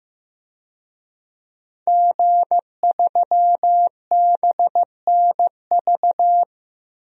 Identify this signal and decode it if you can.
Morse code, heard as a keyed tone, about 15 wpm, 700 Hz. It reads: G3BNV
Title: DXCC-100 Call Signs - 5 Characters - Repeated 3x - 15wpm